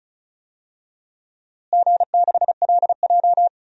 Morse code, heard as a keyed tone, about 36 wpm, 700 Hz. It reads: G6LJ
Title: DXCC-100 Call Signs - 4 Characters - Repeated 3x - 35wpm